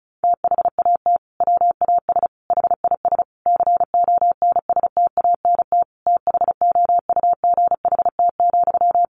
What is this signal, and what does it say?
Morse code, heard as a keyed tone, about 34 wpm, 700 Hz. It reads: THAT WAS HIS CONSTANT THOUGHT,
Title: Sentences from Top 1500 Words 35wpm